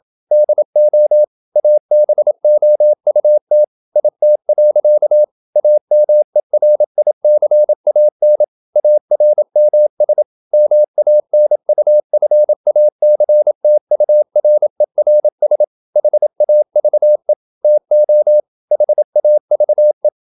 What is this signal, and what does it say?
Morse code, heard as a keyed tone, about 27 wpm, 600 Hz. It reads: DO ABOUT IT. AMERICAN ARMS MANUFACTURERS HAVE TO HAVE
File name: Never 0027wpm